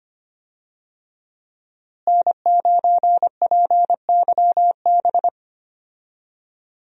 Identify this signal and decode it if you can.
Morse code, heard as a keyed tone, about 25 wpm, 700 Hz. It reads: N9PYB
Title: USA Call Signs - 5 Characters - Repeated 3x - 25wpm